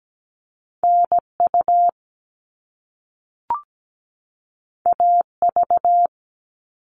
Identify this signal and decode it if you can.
Morse code, heard as a keyed tone, about 17 wpm, 700 Hz. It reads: NU AV
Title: All Cut Numbers - 2-Digit 17wpm